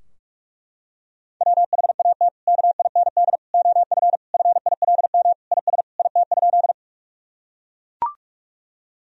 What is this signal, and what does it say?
Morse code, heard as a keyed tone, about 42 wpm, 700 Hz. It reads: WHAT KIND OF FILM IS IT?
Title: Sentences from Top 1000 Words - Review Part 3 - 45wpm